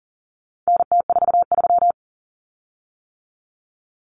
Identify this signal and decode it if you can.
Morse code, heard as a keyed tone, about 40 wpm, 700 Hz. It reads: NT43
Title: Common Cut Numbers - 4-Digit 40wpm